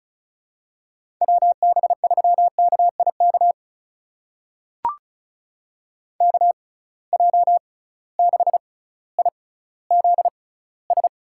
Morse code, heard as a keyed tone, about 36 wpm, spaced as for 18 wpm, 700 Hz. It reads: WB3KIK KJ6IZS
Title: USA Call Signs - 6 Characters - 6x Character Spacing - 35wpm